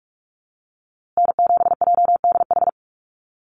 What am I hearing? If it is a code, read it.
Morse code, heard as a keyed tone, about 46 wpm, 700 Hz. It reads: N7JDH
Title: USA Call Signs Repeated 2x 45wpm